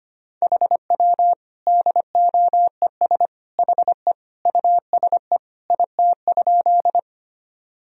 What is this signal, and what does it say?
Morse code, heard as a keyed tone, about 25 wpm, 700 Hz. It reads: HW DOES HE USE IT?
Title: Sentences from Top 200 Words and Abbreviations 25wpm